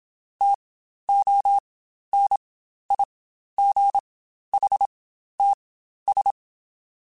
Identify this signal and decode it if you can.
Morse code, heard as a keyed tone, about 27 wpm, spaced as for 12 wpm, 800 Hz. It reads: TONIGHTS
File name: Spaced Characters / Spaced Characters-mp3s / Weather 3 17 sp @25